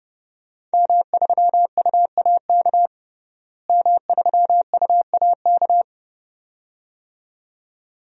Morse code, heard as a keyed tone, about 29 wpm, 700 Hz. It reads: M3UAK M3UAK
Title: DXCC-100 Call Signs - 5 Characters - Repeated 2x - 30wpm